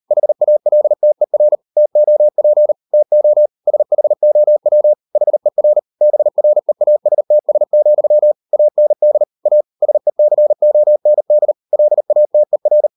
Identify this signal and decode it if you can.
Morse code, heard as a keyed tone, about 38 wpm, 600 Hz. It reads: HALTER TOP TO SHOW HER BREASTS, AND A SECOND LATER